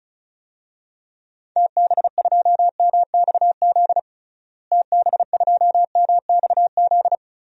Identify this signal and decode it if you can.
Morse code, heard as a keyed tone, about 34 wpm, 700 Hz. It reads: TB2MXZ TB2MXZ
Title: DXCC-100 Call Signs - 6 Characters - Repeated 2x - 35wpm